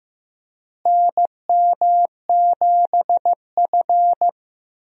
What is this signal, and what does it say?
Morse code, heard as a keyed tone, about 15 wpm, 700 Hz. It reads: NM7F
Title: USA Call Signs - 4 Characters - Repeated 3x - 15wpm